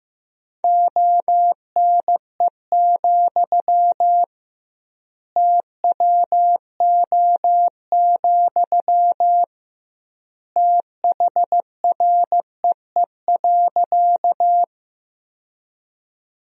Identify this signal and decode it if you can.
Morse code, heard as a keyed tone, about 16 wpm, 700 Hz. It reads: ONE, TWO, THREE.
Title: 2x Word-Spacing - Sentences from Top 200 Words - 15wpm